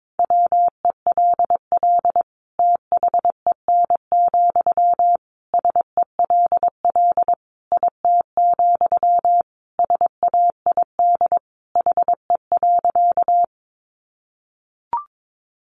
Morse code, heard as a keyed tone, about 22 wpm, 700 Hz. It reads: WELL THEN, SELL IT, SAID HE.
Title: Sentences from Top 1100 Words 22wpm